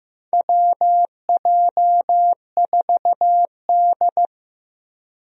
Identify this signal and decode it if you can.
Morse code, heard as a keyed tone, about 14 wpm, 700 Hz. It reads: WJ4D
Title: USA Call Signs - 4 Characters - Repeated 3x - 15wpm